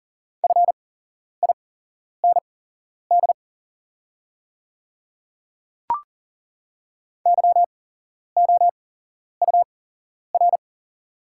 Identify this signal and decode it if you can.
Morse code, heard as a keyed tone, about 40 wpm, spaced as for 11 wpm, 700 Hz. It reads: FIND YOUR
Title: 8x Character Spacing - Top 100 Words - 40wpm